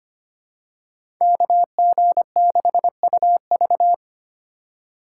Morse code, heard as a keyed tone, about 25 wpm, 700 Hz. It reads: KG6UV
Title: USA Call Signs - 5 Characters - 3x Character Spacing - 25wpm